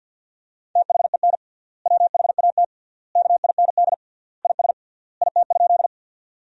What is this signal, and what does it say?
Morse code, heard as a keyed tone, about 50 wpm, 700 Hz. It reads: THEN WHAT KIND IS IT?
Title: 3x Word-Spacing - Sentences from Top 200 Words - 50wpm